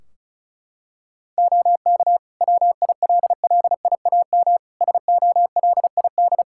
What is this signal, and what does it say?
Morse code, heard as a keyed tone, about 34 wpm, 700 Hz. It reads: OK WILLIAM SOLID
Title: Practice QSO - Exchange 3 - More Info - 35wpm